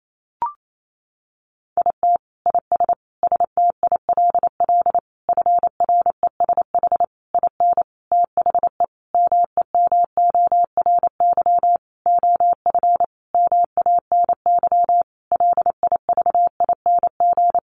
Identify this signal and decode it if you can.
Morse code, heard as a keyed tone, about 28 wpm, 700 Hz. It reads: IT IS STILL FRESH IN THE MEMORY OF MANY LIVING
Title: Sentences from Top 1300 Words 28wpm